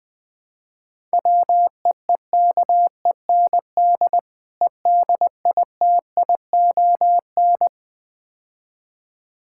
Morse code, heard as a keyed tone, about 20 wpm, 700 Hz. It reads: WEEKEND EDITION
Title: Build A Phrase - 5-Word Phrases - 20wpm